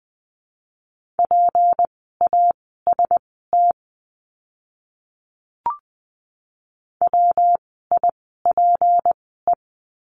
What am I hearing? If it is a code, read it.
Morse code, heard as a keyed tone, about 20 wpm, spaced as for 14 wpm, 700 Hz. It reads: PAST WIPE
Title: 153 Lesson 25f - Letter p - Words 20wpm